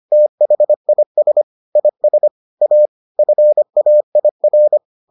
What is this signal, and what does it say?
Morse code, heard as a keyed tone, about 25 wpm, 600 Hz. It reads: THIS IS A FAIR